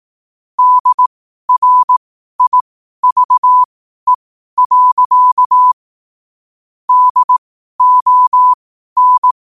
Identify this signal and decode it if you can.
Morse code, heard as a keyed tone, about 18 wpm, spaced as for 12 wpm, 1000 Hz. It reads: DRIVE. DON
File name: qotdfiles / 12wpm